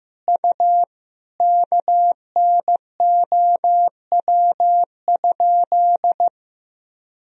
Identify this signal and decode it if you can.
Morse code, heard as a keyed tone, about 15 wpm, 700 Hz. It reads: U KNOW?
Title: Sentences from Top 500 Words and Abbreviations 15wpm